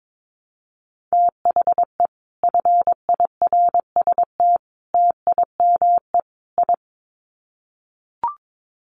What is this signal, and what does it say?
Morse code, heard as a keyed tone, about 22 wpm, 700 Hz. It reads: THE FIRST TIME I
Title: Sets of 4 Words 22wpm